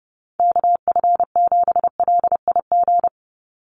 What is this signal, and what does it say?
Morse code, heard as a keyed tone, about 29 wpm, 700 Hz. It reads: KF7LIG
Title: USA Call Signs 30wpm